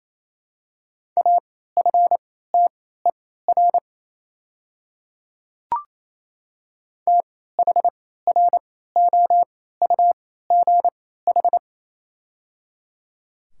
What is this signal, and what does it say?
Morse code, heard as a keyed tone, about 28 wpm, spaced as for 16 wpm, 700 Hz. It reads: AFTER THROUGH
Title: 3x Character Spacing - Top 200 Words - 28wpm